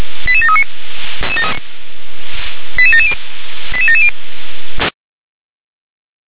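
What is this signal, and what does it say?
Five-tone selective call (ZVEI), heard as quick runs of five tones, tones of about 70 ms. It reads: 88739 88770 88770
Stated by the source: ZVEI five-tone selective call received by SDR.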